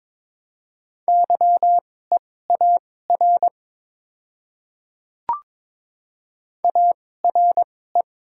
Morse code, heard as a keyed tone, about 21 wpm, spaced as for 16 wpm, 700 Hz. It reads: YEAR ARE
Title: 2x Character Spacing - Top 300 Words - 22wpm